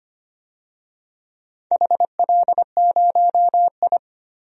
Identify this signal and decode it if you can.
Morse code, heard as a keyed tone, about 25 wpm, 700 Hz. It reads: HL0I
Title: DXCC-100 Call Signs - 4 Characters - 25wpm